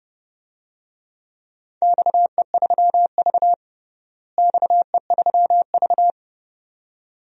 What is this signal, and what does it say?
Morse code, heard as a keyed tone, about 32 wpm, 700 Hz. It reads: XE3V XE3V
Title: DXCC-100 Call Signs - 4 Characters - Repeated 2x - 30wpm